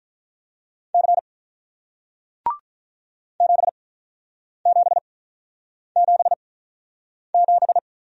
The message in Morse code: C 7777

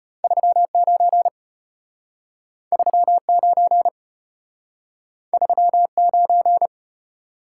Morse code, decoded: 39 39 39

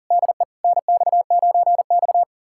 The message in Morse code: DE NX9X

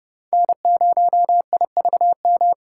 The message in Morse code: N0IVM

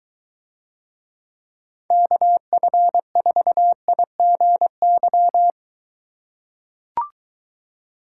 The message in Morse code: KF4IGY

